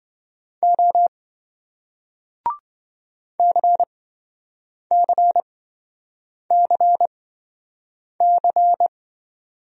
Morse code, heard as a keyed tone, about 26 wpm, spaced as for 7 wpm, 700 Hz. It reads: O CCCC